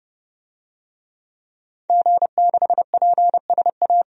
Morse code, heard as a keyed tone, about 31 wpm, 700 Hz. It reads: G6PSA
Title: DXCC-100 Call Signs - 5 Characters - 30wpm